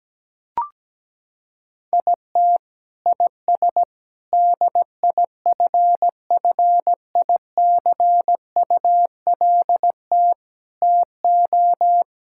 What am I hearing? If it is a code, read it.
Morse code, heard as a keyed tone, about 17 wpm, 700 Hz. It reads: IT IS DIFFICULT TO
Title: Sentences from Top 1000 Words - Review Part 6 - 17wpm